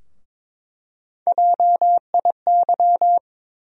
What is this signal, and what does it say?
Morse code, heard as a keyed tone, about 22 wpm, 700 Hz. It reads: JIY
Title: USA Call Sign Postfixes - 3 Letters 22wpm